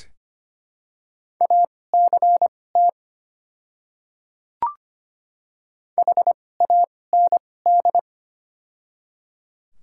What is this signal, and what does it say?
Morse code, heard as a keyed tone, about 25 wpm, spaced as for 18 wpm, 690 Hz. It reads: ACT HAND
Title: 2x Character Spacing - Top 300 Words - 25wpm